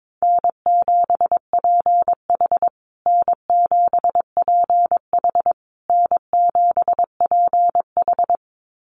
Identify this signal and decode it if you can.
Morse code, heard as a keyed tone, about 22 wpm, 700 Hz. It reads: N7PH N7PH N7PH
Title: USA Call Signs Repeated 3x 22wpm